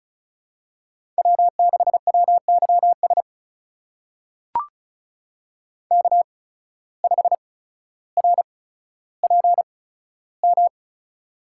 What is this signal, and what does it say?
Morse code, heard as a keyed tone, about 36 wpm, spaced as for 15 wpm, 700 Hz. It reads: W6WYS K5RPM
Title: USA Call Signs - 5 Characters - 8x Character Spacing - 35wpm